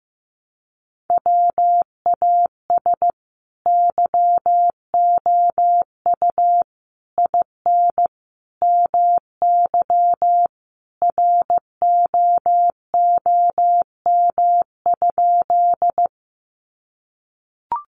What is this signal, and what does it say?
Morse code, heard as a keyed tone, about 15 wpm, 700 Hz. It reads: WAS YOU IN MY ROOM?
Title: Sentences from Top 300 Words 15wpm